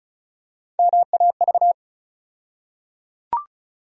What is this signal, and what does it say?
Morse code, heard as a keyed tone, about 36 wpm, 700 Hz. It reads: MAV
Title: USA Call Sign Postfixes - 3 Letters 35wpm